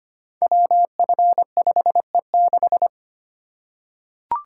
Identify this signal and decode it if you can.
Morse code, heard as a keyed tone, about 25 wpm, 700 Hz. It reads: WF5E6
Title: Code Groups - 5 Letter:Numbers - 25wpm